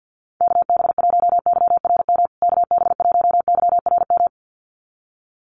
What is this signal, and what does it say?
Morse code, heard as a keyed tone, about 50 wpm, 700 Hz. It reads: KB1YRM KB1YRM
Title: USA Call Signs Repeated 2x 50wpm